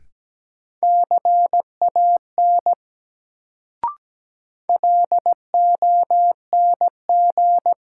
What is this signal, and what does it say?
Morse code, heard as a keyed tone, about 17 wpm, 700 Hz. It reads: CAN LONG
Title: Sentences from Top 100 Words - New Word Drill - 17wpm